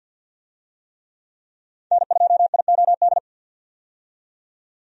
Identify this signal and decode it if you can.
Morse code, heard as a keyed tone, about 53 wpm, 700 Hz. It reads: N2IOD
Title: USA Call Signs - 5 Characters - 50wpm